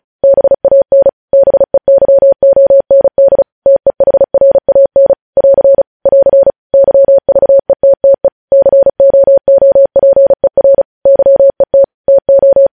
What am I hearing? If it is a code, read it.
Morse code, heard as a keyed tone, about 36 wpm, 550 Hz. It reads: BAN BEYOND TEHRAN <AR> <AR> YVETTE COOPER YET TO
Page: news headlines in morse code at 35 wpm